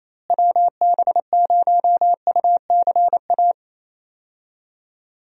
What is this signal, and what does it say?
Morse code, heard as a keyed tone, about 28 wpm, 700 Hz. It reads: WB0UCA
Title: USA Call Signs - 6 Characters - Repeated 2x - 28wpm